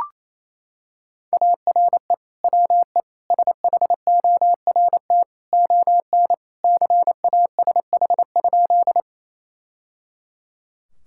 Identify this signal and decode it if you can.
Morse code, heard as a keyed tone, about 28 wpm, 700 Hz. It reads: ARE WE SHORT ON CASH?
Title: Sentences from Top 1000 Words - Review Part 5 - 28wpm